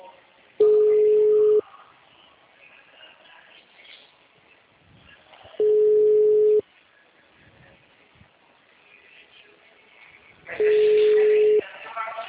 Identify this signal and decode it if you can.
Ringback tone (Europe, 425 Hz), with speech.
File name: out- -2043- - - .52